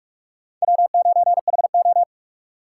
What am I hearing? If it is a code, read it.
Morse code, heard as a keyed tone, about 47 wpm, 700 Hz. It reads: W9HO
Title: USA Call Signs - 4 Characters - 3x Character Spacing - 45wpm